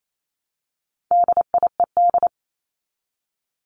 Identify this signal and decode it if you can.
Morse code, heard as a keyed tone, about 28 wpm, 700 Hz. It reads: DIED
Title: ED Suffix - Top 100 Words with Base Words 28wpm